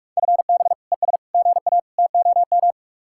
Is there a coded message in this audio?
Yes: Morse code from 0.2 to 2.7 s, heard as a keyed tone, about 57 wpm, 700 Hz. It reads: FB ES GA TOM